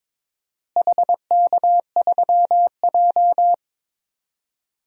Morse code, heard as a keyed tone, about 22 wpm, 700 Hz. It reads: HK3J
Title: DXCC-100 Call Signs - 4 Characters - Repeated 3x - 22wpm